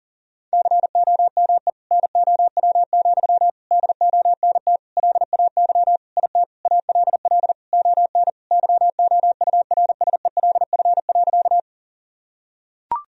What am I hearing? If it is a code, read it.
Morse code, heard as a keyed tone, about 40 wpm, 700 Hz. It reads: COME NOW, DONT LAY IT ALL ON YOURSELF.
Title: Sentences from Top 1000 Words - Review Part 2 - 40wpm